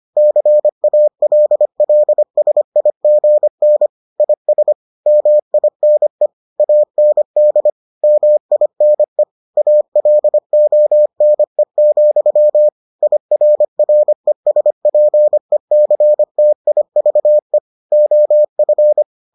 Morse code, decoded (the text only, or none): CALLSIGN IS MINE AND MINE ALONE, IRRESPECTIVE OF